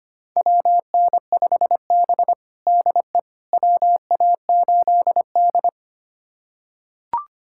Morse code, heard as a keyed tone, about 25 wpm, 700 Hz. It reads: WN5B DE WA8D
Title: USA Callsign DE Callsign - 4 Characters - 25wpm